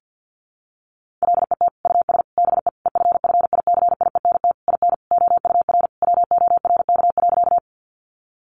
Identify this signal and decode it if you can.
Morse code, heard as a keyed tone, about 48 wpm, 700 Hz. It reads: LET US BE EFFICIENT IN OUR WORK.
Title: Sentences from Top 1300 Words 50wpm